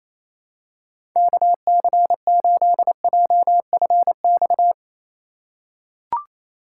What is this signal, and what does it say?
Morse code, heard as a keyed tone, about 28 wpm, 700 Hz. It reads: KC8JFX
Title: USA Call Signs - 4x Character Spacing - 28wpm